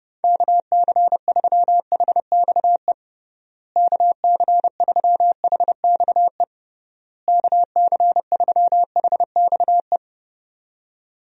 Morse code, heard as a keyed tone, about 30 wpm, 700 Hz. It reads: KC3HXE KC3HXE KC3HXE
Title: USA Call Signs - 6 Characters - Repeated 3x - 30wpm